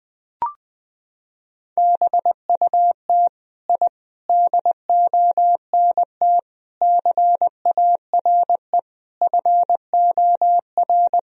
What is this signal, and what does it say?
Morse code, heard as a keyed tone, about 20 wpm, 700 Hz. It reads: BUT I DONT CARE FOR
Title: Sentences from Top 1000 Words - Review Part 1 - 20wpm